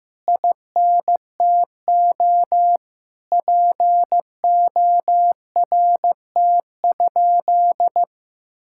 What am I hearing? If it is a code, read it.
Morse code, heard as a keyed tone, about 15 wpm, 700 Hz. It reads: INTO PORT?
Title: Sentences from Top 1000 Words - Review Part 5 - 15wpm